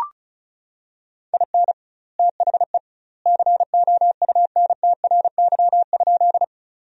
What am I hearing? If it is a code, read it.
Morse code, heard as a keyed tone, about 36 wpm, 700 Hz. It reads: IN THE COUNTRY?
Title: 2x Word-Spacing - Sentences from Top 300 Words - 35wpm